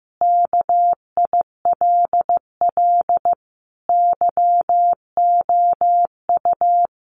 Morse code, heard as a keyed tone, about 15 wpm, 700 Hz. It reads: KILL YOU
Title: A Sound Of Thunder 15wpm